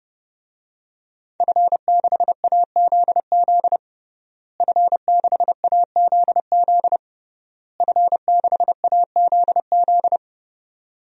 Morse code, heard as a keyed tone, about 31 wpm, 700 Hz. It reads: F6AZZ F6AZZ F6AZZ
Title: DXCC-100 Call Signs - 5 Characters - Repeated 3x - 30wpm